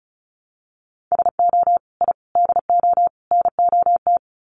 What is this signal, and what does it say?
Morse code, heard as a keyed tone, about 34 wpm, 700 Hz. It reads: SO I DO NOT
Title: Sets of 4 Words 35wpm